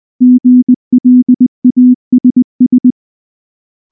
Morse code, heard as a keyed tone, about 19 wpm, 250 Hz. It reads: GLASS
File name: morse / morse-training-20WPM-4